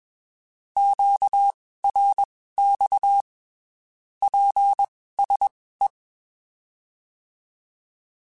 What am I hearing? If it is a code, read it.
Morse code, heard as a keyed tone, about 21 wpm, spaced as for 15 wpm, 800 Hz. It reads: QRX PSE
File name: (spaced) / QSOs Spaced-mp3s / kc7sac @20 sp